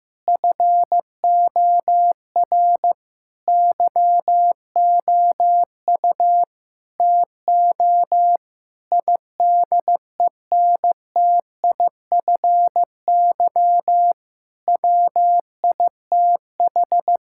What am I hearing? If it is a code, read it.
Morse code, heard as a keyed tone, about 15 wpm, 700 Hz. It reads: FOR YOU TO IDENTIFY WITH
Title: Sentences from Top 1000 Words - Review Part 6 - 15wpm